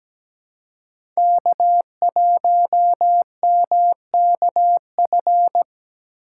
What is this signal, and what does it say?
Morse code, heard as a keyed tone, about 17 wpm, 700 Hz. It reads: K1MKF
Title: USA Call Signs - 4x Character Spacing - 17wpm